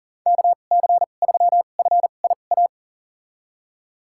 Morse code, heard as a keyed tone, about 40 wpm, spaced as for 29 wpm, 690 Hz. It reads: KC3FIA
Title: USA Call Signs - 6 Characters - 2x Character Spacing - 40wpm